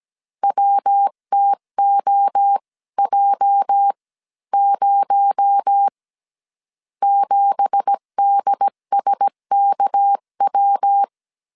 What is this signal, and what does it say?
Morse code, heard as a keyed tone, about 17 wpm, spaced as for 14 wpm, 800 Hz. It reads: WTOJ0 7DSKW
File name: CW / DET / 1 9 1500C S B 16wpm 2856